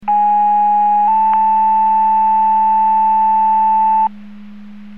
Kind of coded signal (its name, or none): two-tone page